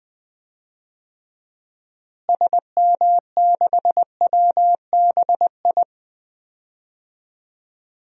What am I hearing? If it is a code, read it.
Morse code, heard as a keyed tone, about 20 wpm, 700 Hz. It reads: SM6WBI
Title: DXCC-100 Call Signs - 6 Characters - Repeated 2x - 20wpm